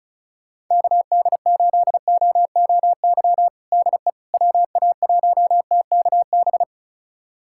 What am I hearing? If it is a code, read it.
Morse code, heard as a keyed tone, about 36 wpm, 700 Hz. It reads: KD8OOY DE WA1TKB
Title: USA Callsign DE Callsign - 6 Characters - 35wpm